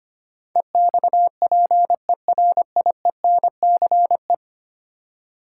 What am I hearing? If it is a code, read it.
Morse code, heard as a keyed tone, about 25 wpm, 700 Hz. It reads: EXPERIENCE
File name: 10 Letter Words - First 3 Letters Given - 25wpm